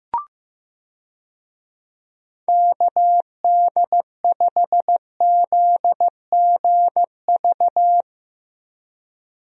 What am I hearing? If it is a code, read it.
Morse code, heard as a keyed tone, about 15 wpm, 700 Hz. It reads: KD5ZGV
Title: USA Call Signs - 6 Characters - Repeated 2x - 15wpm